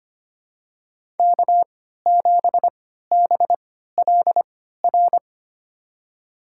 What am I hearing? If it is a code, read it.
Morse code, heard as a keyed tone, about 25 wpm, spaced as for 14 wpm, 700 Hz. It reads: K7BLR
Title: USA Call Signs - 5 Characters - 3x Character Spacing - 25wpm